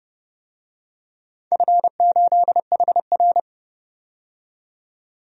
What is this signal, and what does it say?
Morse code, heard as a keyed tone, about 30 wpm, 700 Hz. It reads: F8HR
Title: DXCC-100 Call Signs - 4 Characters - 30wpm